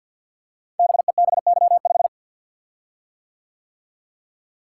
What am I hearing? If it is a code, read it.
Morse code, heard as a keyed tone, about 48 wpm, 700 Hz. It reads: BEBY5
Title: Code Groups - 5 Letter:Numbers - 50wpm